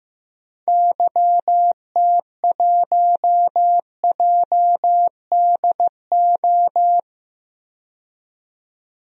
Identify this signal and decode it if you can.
Morse code, heard as a keyed tone, about 15 wpm, 700 Hz. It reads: YT1JDO
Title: DXCC-100 Call Signs - 6 Characters - 15wpm